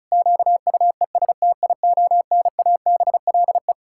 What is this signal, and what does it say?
Morse code, heard as a keyed tone, about 36 wpm, 700 Hz. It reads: QUESTIONABLE
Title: Pangram Challenge v2 - 35wpm